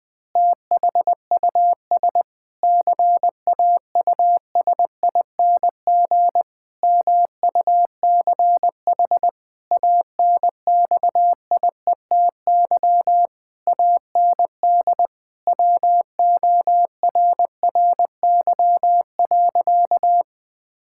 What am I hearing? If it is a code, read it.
Morse code, heard as a keyed tone, about 20 wpm, 700 Hz. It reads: THUS CAUSING MUCH ANXIETY AND WORRY.
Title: Pangram Challenge v2 - 20wpm